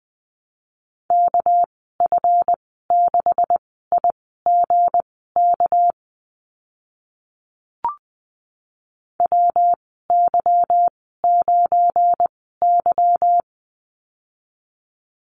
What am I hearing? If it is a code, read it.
Morse code, heard as a keyed tone, about 20 wpm, spaced as for 15 wpm, 700 Hz. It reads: KF6IGK WY9Y
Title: 200 Lesson 32g - Letter k - Call Signs 20wpm